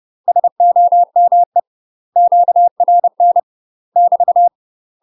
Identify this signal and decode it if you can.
Morse code, heard as a keyed tone, about 30 wpm, 700 Hz. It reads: SOME QRN <BT>
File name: UK MASTER QSOs TRAINING / 30WPM / UK MASTER QSOs 0011